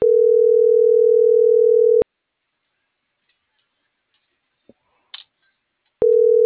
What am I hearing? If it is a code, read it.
Ringback tone (North America), 440 + 480 Hz.